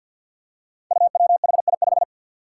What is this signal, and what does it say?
Morse code, heard as a keyed tone, about 45 wpm, 700 Hz. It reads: UWHI5
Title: Five Challenging Letter-Number Combinations 50wpm